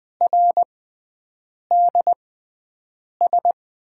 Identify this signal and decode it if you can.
Morse code, heard as a keyed tone, about 20 wpm, spaced as for 7 wpm, 700 Hz. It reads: RDS
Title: USA Call Signs - 6 Characters - 6x Character Spacing - 20wpm